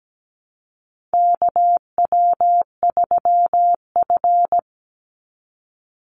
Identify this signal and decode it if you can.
Morse code, heard as a keyed tone, about 17 wpm, 700 Hz. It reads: KW3F
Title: Call Signs - 17wpm